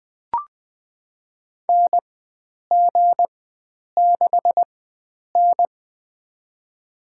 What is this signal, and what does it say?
Morse code, heard as a keyed tone, about 20 wpm, spaced as for 9 wpm, 700 Hz. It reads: NG6N
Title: USA Call Signs - 4 Characters - 4x Character Spacing - 20wpm